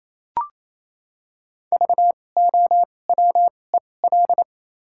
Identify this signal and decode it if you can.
Morse code, heard as a keyed tone, about 28 wpm, spaced as for 20 wpm, 700 Hz. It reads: VOWEL
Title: 2x Character Spacing - Top 500 Words - 28wpm